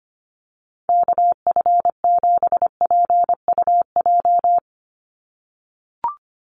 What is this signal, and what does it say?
Morse code, heard as a keyed tone, about 25 wpm, 700 Hz. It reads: KF7PUJ
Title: USA Call Signs 25wpm